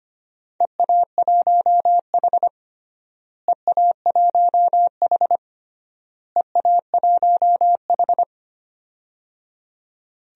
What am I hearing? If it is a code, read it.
Morse code, heard as a keyed tone, about 25 wpm, 700 Hz. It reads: EA1H EA1H EA1H
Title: DXCC-100 Call Signs - 4 Characters - Repeated 3x - 25wpm